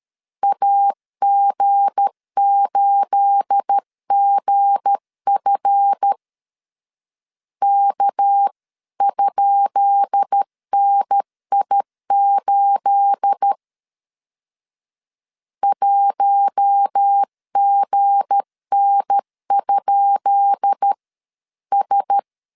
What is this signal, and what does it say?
Morse code, heard as a keyed tone, about 13 wpm, 800 Hz. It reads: AG8GF K?NI8 1GN?S
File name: LEZIONI / 10 SIGNS / 10B INT 12wpm 2414